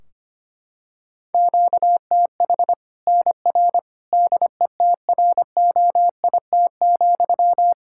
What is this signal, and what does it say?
Morse code, heard as a keyed tone, about 25 wpm, 700 Hz. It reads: QTH NR DETROIT,
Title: Practice QSO - Exchange 1 - Beginning - 25wpm